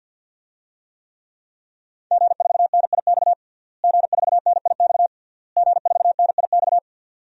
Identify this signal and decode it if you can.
Morse code, heard as a keyed tone, about 51 wpm, 700 Hz. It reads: G4NIX G4NIX G4NIX
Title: DXCC-100 Call Signs - 5 Characters - Repeated 3x - 50wpm